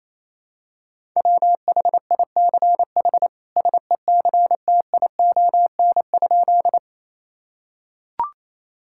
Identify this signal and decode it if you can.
Morse code, heard as a keyed tone, about 28 wpm, 700 Hz. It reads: WHICH SECTION?
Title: Sentences from Top 1000 Words - Review Part 5 - 28wpm